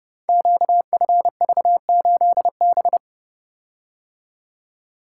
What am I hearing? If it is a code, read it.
Morse code, heard as a keyed tone, about 29 wpm, 700 Hz. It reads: QFV8B
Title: Code Groups - 5 Letter:Numbers - 30wpm